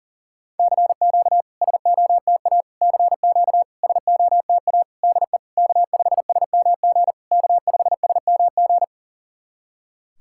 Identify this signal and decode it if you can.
Morse code, heard as a keyed tone, about 40 wpm, 690 Hz. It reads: CQ SOTA CQ SOTA DE K5SMG K5SMG
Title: SOTA QSO Practice 40wpm